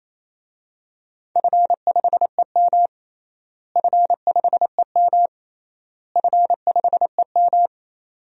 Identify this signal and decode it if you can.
Morse code, heard as a keyed tone, about 28 wpm, 700 Hz. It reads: F5EM F5EM F5EM
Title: DXCC-100 Call Signs - 4 Characters - Repeated 3x - 28wpm